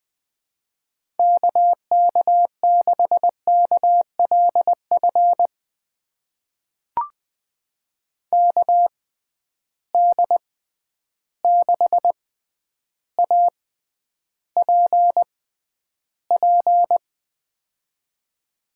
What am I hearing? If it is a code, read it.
Morse code, heard as a keyed tone, about 20 wpm, spaced as for 10 wpm, 700 Hz. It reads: KK6KLF KD6APP
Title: USA Call Signs - 6 Characters - 6x Character Spacing - 20wpm